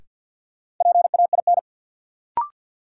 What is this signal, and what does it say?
Morse code, heard as a keyed tone, about 53 wpm, 700 Hz. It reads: PAIN
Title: Sentences from Top 1600 Words - New Word Drill - 50wpm